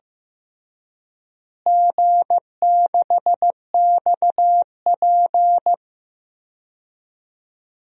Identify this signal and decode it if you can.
Morse code, heard as a keyed tone, about 15 wpm, 700 Hz. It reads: G6XP
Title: DXCC-100 Call Signs - 4 Characters - Repeated 3x - 15wpm